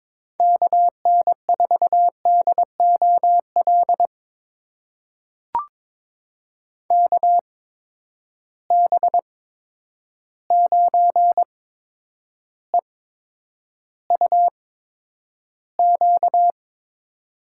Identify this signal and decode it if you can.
Morse code, heard as a keyed tone, about 22 wpm, spaced as for 9 wpm, 700 Hz. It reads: KN4DOL KB9EUQ
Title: USA Call Signs - 6 Characters - 8x Character Spacing - 22wpm